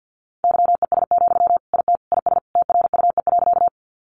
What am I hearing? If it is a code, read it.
Morse code, heard as a keyed tone, about 50 wpm, 700 Hz. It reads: YES, IT IS TRUE.
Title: Sentences from Top 500 Words 50wpm